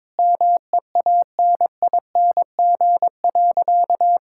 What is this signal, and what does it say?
Morse code, heard as a keyed tone, about 22 wpm, 700 Hz. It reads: MEANING.
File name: practice / 22wpm / Sentences with 2-Word Phrases - 22wpm